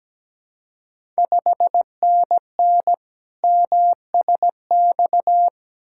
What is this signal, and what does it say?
Morse code, heard as a keyed tone, about 17 wpm, 700 Hz. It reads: 5NN MSX
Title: Ontario QSO Party Exchange 17wpm